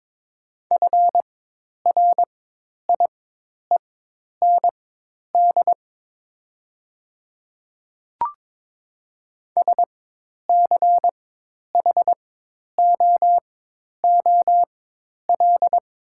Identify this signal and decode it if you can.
Morse code, heard as a keyed tone, about 22 wpm, spaced as for 10 wpm, 700 Hz. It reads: FRIEND SCHOOL
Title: 4x Character Spacing - Top 300 Words - 22wpm